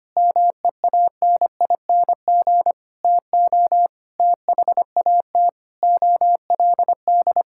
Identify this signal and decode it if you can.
Morse code, heard as a keyed tone, about 25 wpm, 700 Hz. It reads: MEANING TO THAT OLD
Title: Sentences from Top 1600 Words 25wpm